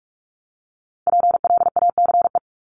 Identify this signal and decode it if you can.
Morse code, heard as a keyed tone, about 44 wpm, 700 Hz. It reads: PLACE
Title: Top 500 Words 45wpm